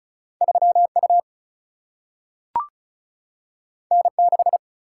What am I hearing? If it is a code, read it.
Morse code, heard as a keyed tone, about 36 wpm, 700 Hz. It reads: 3 U N 6
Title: Single-Letter Single-Number 35wpm